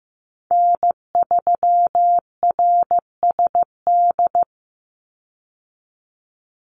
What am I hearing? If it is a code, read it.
Morse code, heard as a keyed tone, about 15 wpm, 700 Hz. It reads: N3RSD
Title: Call Signs Repeated 3x - 15wpm